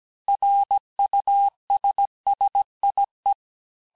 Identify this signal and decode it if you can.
Morse code, heard as a keyed tone, about 17 wpm, 800 Hz. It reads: RUSSIE
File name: cours-du-lundi files / cours-1587-texte-clair-vitesse-16mpm-puis-21mpm